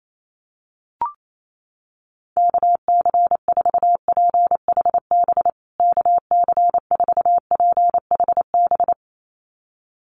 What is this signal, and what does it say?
Morse code, heard as a keyed tone, about 28 wpm, 700 Hz. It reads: KC4PHB KC4PHB
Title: USA Call Signs Repeated 2x 28wpm